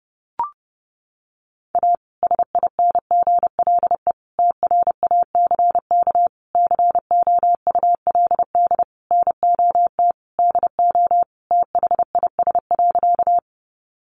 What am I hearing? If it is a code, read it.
Morse code, heard as a keyed tone, about 30 wpm, 700 Hz. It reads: A SINGLE TRACK COULD NOT DO THIS.
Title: Sentences from Top 1100 Words 30wpm